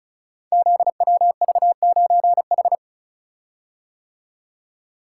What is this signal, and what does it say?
Morse code, heard as a keyed tone, about 36 wpm, 700 Hz. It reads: ZWV9H
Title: Code Groups - 5 Letter:Numbers - 35wpm